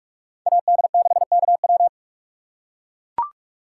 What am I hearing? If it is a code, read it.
Morse code, heard as a keyed tone, about 43 wpm, 700 Hz. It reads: AD6KW